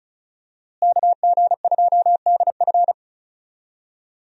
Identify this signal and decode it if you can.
Morse code, heard as a keyed tone, about 34 wpm, 700 Hz. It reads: KG2DF
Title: USA Call Signs - 5 Characters - Repeated 3x - 35wpm